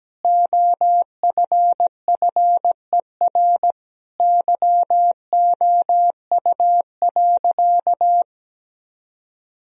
Morse code, heard as a keyed tone, about 17 wpm, 700 Hz. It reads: OFFER YOU.
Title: Sentences from Top 1000 Words - Review Part 4 - 17wpm